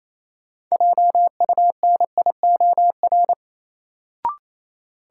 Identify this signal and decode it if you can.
Morse code, heard as a keyed tone, about 28 wpm, 700 Hz. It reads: JUNIOR
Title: Sentences from Top 1000 Words - Review Part 7 - New Word Drill - 28wpm